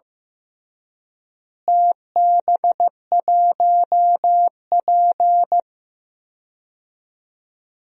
Morse code, heard as a keyed tone, about 14 wpm, 700 Hz. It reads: TB1P
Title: DXCC-100 Call Signs - 4 Characters - Repeated 3x - 15wpm